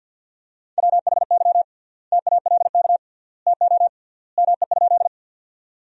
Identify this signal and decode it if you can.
Morse code, heard as a keyed tone, about 50 wpm, 700 Hz. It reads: WHY TALK TO ME?
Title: 3x Word-Spacing - Sentences from Top 400 Words - 50wpm